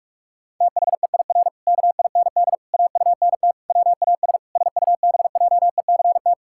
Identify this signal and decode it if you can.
Morse code, heard as a keyed tone, about 46 wpm, 700 Hz. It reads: THEIR KIND AUNT WAS SUBJECT
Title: Pangram Challenge v2 - 45wpm